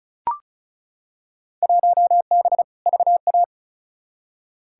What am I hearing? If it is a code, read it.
Morse code, heard as a keyed tone, about 34 wpm, 700 Hz. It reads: 1B VA
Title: Field Day Exchanges - 35wpm